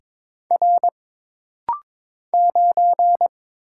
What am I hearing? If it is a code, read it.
Morse code, heard as a keyed tone, about 22 wpm, 700 Hz. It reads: R 9
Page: This rapid-fire timing is geared to push your ICR (Instant Character Recognition) skills.